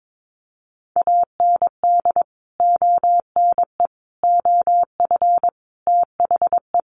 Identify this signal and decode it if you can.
Morse code, heard as a keyed tone, about 22 wpm, 700 Hz. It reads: AND ONE OF THE
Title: Sets of 5 Words 22wpm